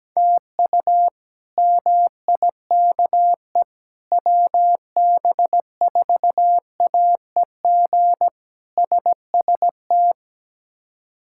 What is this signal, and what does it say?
Morse code, heard as a keyed tone, about 17 wpm, 700 Hz. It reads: TU MIKE WB4AEG SST
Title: K1USN SST QSO Practice - 17wpm